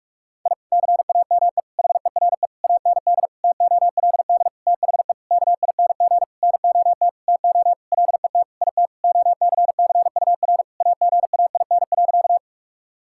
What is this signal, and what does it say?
Morse code, heard as a keyed tone, about 46 wpm, 700 Hz. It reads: I CAME HERE AND TOLD THE KING NOT TO LET IT OCCUR AGAIN.
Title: Sentences from Top 1000 Words - Review Part 6 - 45wpm